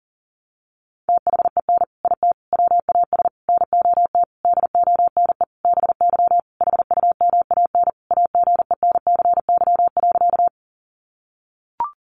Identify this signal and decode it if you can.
Morse code, heard as a keyed tone, about 40 wpm, 700 Hz. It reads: THEN IT WAS NOT DONE BY HUMAN AGENCY.
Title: Sentences from Top 1300 Words 40wpm